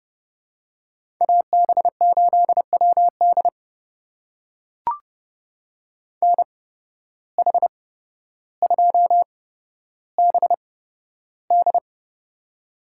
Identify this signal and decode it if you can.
Morse code, heard as a keyed tone, about 30 wpm, spaced as for 13 wpm, 700 Hz. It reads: AB8WD NH2BD
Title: USA Call Signs - 5 Characters - 8x Character Spacing - 30wpm